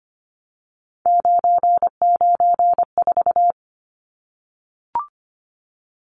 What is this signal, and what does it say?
Morse code, heard as a keyed tone, about 25 wpm, 700 Hz. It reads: 994
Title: Numbers - 3-Digit 25wpm